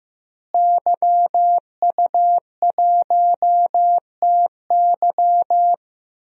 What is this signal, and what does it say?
Morse code, heard as a keyed tone, about 16 wpm, 700 Hz. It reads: YU1TY
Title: DXCC-100 Call Signs - 5 Characters - Repeated 2x - 15wpm